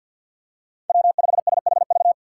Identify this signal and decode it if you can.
Morse code, heard as a keyed tone, about 50 wpm, 700 Hz. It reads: W5SHV
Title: USA Call Signs - 5 Characters - 3x Character Spacing - 50wpm